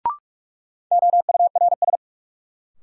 Morse code, heard as a keyed tone, about 42 wpm, 700 Hz. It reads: OURS
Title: Sentences from Top 1300 Words - New Word Drill Alt - 45wpm